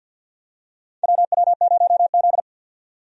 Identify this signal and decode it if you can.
Morse code, heard as a keyed tone, about 52 wpm, 700 Hz. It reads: WW0Z
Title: USA Call Signs - 4 Characters - 50wpm